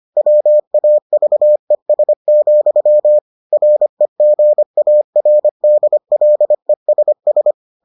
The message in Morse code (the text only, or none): WAVES, REGARDLESS